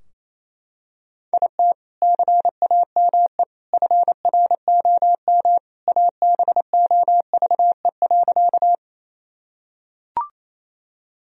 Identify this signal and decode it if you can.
Morse code, heard as a keyed tone, about 28 wpm, 700 Hz. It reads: IT CAME FROM ABOVE.
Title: Sentences from Top 1000 Words - Review Part 2 - 28wpm